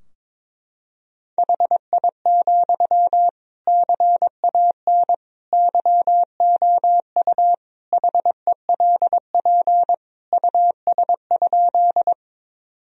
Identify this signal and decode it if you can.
Morse code, heard as a keyed tone, about 22 wpm, 700 Hz. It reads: HI, CAN YOU HELP US?
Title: Sentences from Top 1000 Words - Review Part 3 - 22wpm